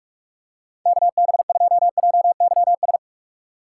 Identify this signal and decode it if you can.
Morse code, heard as a keyed tone, about 46 wpm, 700 Hz. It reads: KB2JYS
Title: USA Call Signs - 6 Characters - 4x Character Spacing - 45wpm